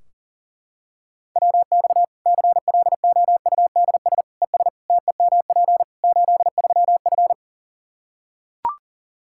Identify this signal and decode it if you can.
Morse code, heard as a keyed tone, about 40 wpm, 700 Hz. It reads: WX CLOUDS ES TEMP 83F
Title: Practice QSO - Exchange 2 - Info - 40wpm